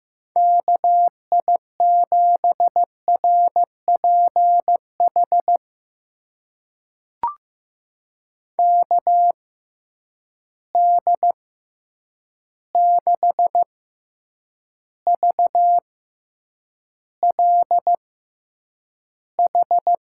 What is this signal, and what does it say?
Morse code, heard as a keyed tone, about 15 wpm, spaced as for 8 wpm, 700 Hz. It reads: KI7RPH KD6VLH
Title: USA Call Signs - 6 Characters - 6x Character Spacing - 15wpm